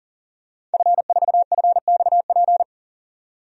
Morse code, heard as a keyed tone, about 37 wpm, 700 Hz. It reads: F4FXP
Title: DXCC-100 Call Signs - 5 Characters - 40wpm